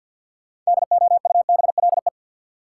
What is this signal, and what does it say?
Morse code, heard as a keyed tone, about 53 wpm, 700 Hz. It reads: DOUBLE
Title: Sentences from Top 1400 Words - New Word Drill - 50wpm